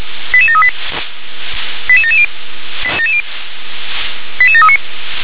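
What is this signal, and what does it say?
Five-tone selective call (ZVEI), heard as quick runs of five tones, tones of about 70 ms. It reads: 88738 88770 88739
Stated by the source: ZVEI five-tone selective call 88739, received via SDR.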